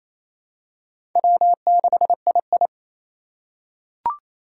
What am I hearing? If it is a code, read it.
Morse code, heard as a keyed tone, about 28 wpm, 700 Hz. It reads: W6II
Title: USA Call Signs - 3x Character Spacing - 28wpm